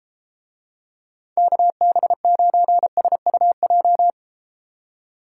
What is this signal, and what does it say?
Morse code, heard as a keyed tone, about 32 wpm, 700 Hz. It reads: KB9SUJ